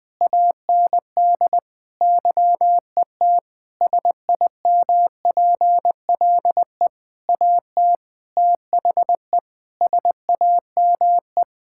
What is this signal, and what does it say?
Morse code, heard as a keyed tone, about 20 wpm, 700 Hz. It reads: AND YET SIMPLE AT THE SAME
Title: Sentences from Top 1000 Words - Review Part 6 - 20wpm